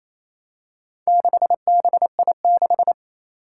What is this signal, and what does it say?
Morse code, heard as a keyed tone, about 28 wpm, 700 Hz. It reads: 6BI6
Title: Four Challenging Letter-Number Combinations 28wpm